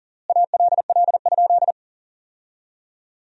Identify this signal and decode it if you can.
Morse code, heard as a keyed tone, about 37 wpm, 700 Hz. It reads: ALL?
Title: 4x Word-Spacing - Sentences from Top 200 Words - 40wpm